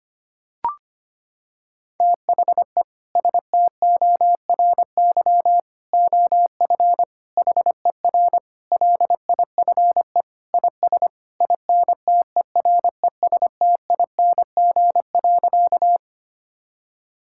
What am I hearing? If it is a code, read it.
Morse code, heard as a keyed tone, about 25 wpm, 700 Hz. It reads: THE STORY OF HER LIFE IS INTERESTING.
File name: practice / 25wpm / Sentences with 3-Word Phrases - 25wpm